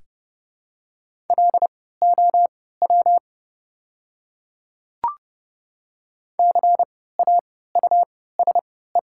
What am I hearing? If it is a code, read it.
Morse code, heard as a keyed tone, about 30 wpm, spaced as for 17 wpm, 690 Hz. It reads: LOW CAUSE
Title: 3x Character Spacing - Top 200 Words - 30wpm